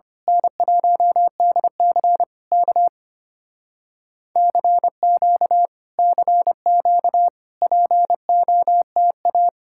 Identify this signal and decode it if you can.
Morse code, heard as a keyed tone, about 24 wpm, 700 Hz. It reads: N1DC K CQ CQ POTA
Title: Speed-Racing Common QSO Phrases - 25wpm